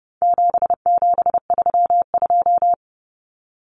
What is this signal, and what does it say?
Morse code, heard as a keyed tone, about 30 wpm, 700 Hz. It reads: 7732
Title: Common Cut Numbers - 4-Digit 30wpm